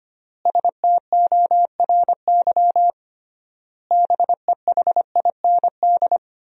Morse code, heard as a keyed tone, about 25 wpm, 700 Hz. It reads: STORY BEHIND
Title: 3x Word-Spacing - Sentences from Top 500 Words - 25wpm